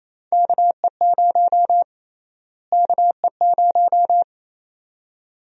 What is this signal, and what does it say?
Morse code, heard as a keyed tone, about 29 wpm, 700 Hz. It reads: KE0 KE0
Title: USA Call Sign Prefixes - Repeated 2x - 28wpm